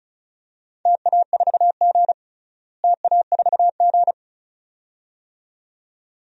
Morse code, heard as a keyed tone, about 34 wpm, 700 Hz. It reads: TA4G TA4G
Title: DXCC-100 Call Signs - 4 Characters - Repeated 2x - 35wpm